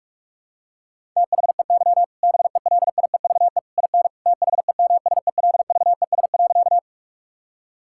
Morse code, heard as a keyed tone, about 41 wpm, 700 Hz. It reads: THEY BELIEVE IN THEMSELVES.
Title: Sentences from Top 1000 Words - Review Part 5 - 45wpm